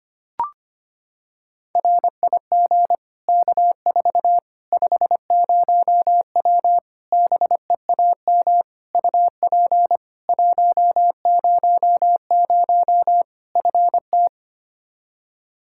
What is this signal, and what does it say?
Morse code, heard as a keyed tone, about 25 wpm, 700 Hz. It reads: RIG K4 50W BEAM UP 100 FT
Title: Practice QSO - Exchange 2 - Info - 25wpm